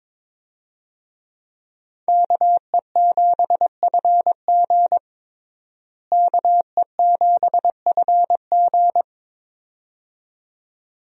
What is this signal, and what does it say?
Morse code, heard as a keyed tone, about 22 wpm, 700 Hz. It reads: KE7FG KE7FG
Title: USA Call Signs - 5 Characters - Repeated 2x - 22wpm